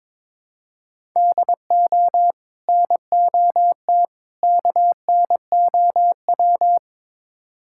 Morse code, heard as a keyed tone, about 21 wpm, 700 Hz. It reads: DO NOT KNOW
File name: practice / 22wpm / Build A Phrase - 5-Word Phrases - 22wpm